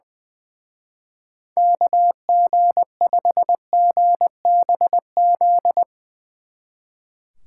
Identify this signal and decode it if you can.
Morse code, heard as a keyed tone, about 22 wpm, 700 Hz. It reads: KG5GBZ
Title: Speed Racing - USA 2x3 Call Signs 20wpm